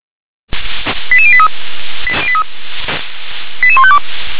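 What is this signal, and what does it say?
Five-tone selective call (ZVEI), heard as quick runs of five tones, tones of about 70 ms. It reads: 88083 88152
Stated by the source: Five-tone ZVEI selective call received by SDR.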